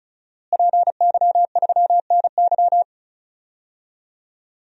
Morse code, heard as a keyed tone, about 36 wpm, 700 Hz. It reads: PY3NY
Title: DXCC-100 Call Signs - 5 Characters - Repeated 2x - 35wpm